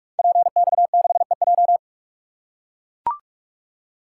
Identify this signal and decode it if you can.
Morse code, heard as a keyed tone, about 44 wpm, 700 Hz. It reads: PX6EJ
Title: Code Groups - 5 Letter:Numbers - 45wpm